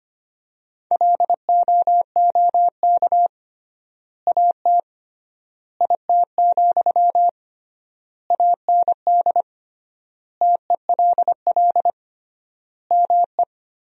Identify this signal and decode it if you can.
Morse code, heard as a keyed tone, about 25 wpm, 700 Hz. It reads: LOOK AT IT, AND TELL ME
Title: 3x Word-Spacing - Sentences from Top 200 Words - 25wpm